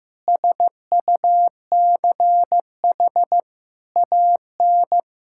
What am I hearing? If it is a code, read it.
Morse code, heard as a keyed tone, about 15 wpm, 700 Hz. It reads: SUCH AN
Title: Sentences from Top 1600 Words 15wpm